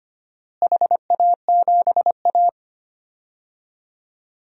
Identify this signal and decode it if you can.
Morse code, heard as a keyed tone, about 25 wpm, 700 Hz. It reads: HA7A
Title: DXCC-100 Call Signs - 4 Characters - Repeated 3x - 25wpm